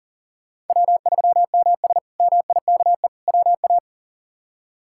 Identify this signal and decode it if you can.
Morse code, heard as a keyed tone, about 40 wpm, 700 Hz. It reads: W3MS MIKE WA
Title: K1USN SST QSO Practice - 40wpm